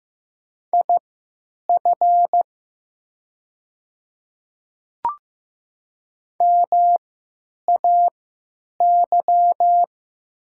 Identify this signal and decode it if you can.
Morse code, heard as a keyed tone, about 15 wpm, spaced as for 9 wpm, 690 Hz. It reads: IF MAY
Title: 3x Character Spacing - Top 300 Words - 15wpm